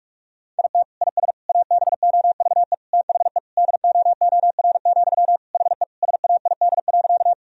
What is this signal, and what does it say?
Morse code, heard as a keyed tone, about 42 wpm, 700 Hz. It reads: IT IS ABOVE THE DOOR, HE SAID.
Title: Sentences from Top 1000 Words - Review Part 2 - 45wpm